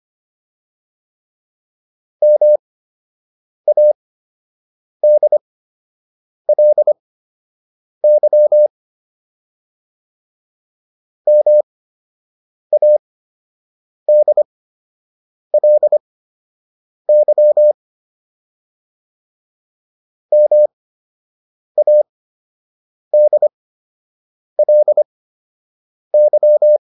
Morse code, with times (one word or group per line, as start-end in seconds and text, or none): 2.2-8.7 MADLY
11.3-17.7 MADLY
20.3-26.8 MADLY